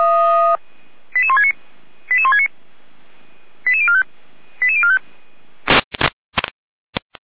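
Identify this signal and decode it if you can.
Five-tone selective call (ZVEI), heard as quick runs of five tones, tones of about 80 ms. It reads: 88169 88169 88045 88045
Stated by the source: ZVEI five-tone selective call received on an SDR.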